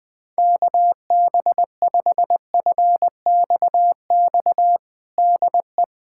Morse code, heard as a keyed tone, about 21 wpm, 700 Hz. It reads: KB5FXX DE
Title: USA Callsign DE Callsign - 6 Characters - 20wpm